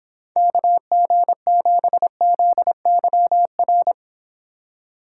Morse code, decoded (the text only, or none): KG7ZYR